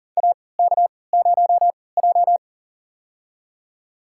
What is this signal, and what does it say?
Morse code, heard as a keyed tone, about 40 wpm, spaced as for 23 wpm, 700 Hz. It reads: AK0J
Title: USA Call Signs - 4 Characters - 3x Character Spacing - 40wpm